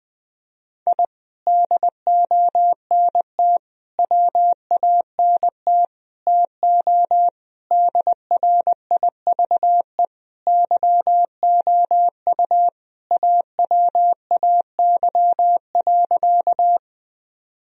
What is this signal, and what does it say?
Morse code, heard as a keyed tone, about 20 wpm, 700 Hz. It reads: I DONT WANT TO DRIVE YOU AWAY.
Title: Sentences from Top 1000 Words - Review Part 2 - 20wpm